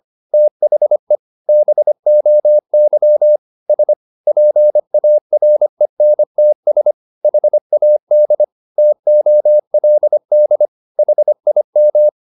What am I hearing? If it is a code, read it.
Morse code, heard as a keyed tone, about 25 wpm, 600 Hz. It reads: THE BOY S PARENTS HAD TOLD HIM